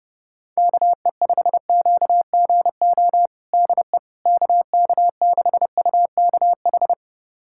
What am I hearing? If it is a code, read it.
Morse code, heard as a keyed tone, about 30 wpm, 700 Hz. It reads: KE5QGO DE KK6UKH
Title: USA Callsign DE Callsign - 6 Characters - 30wpm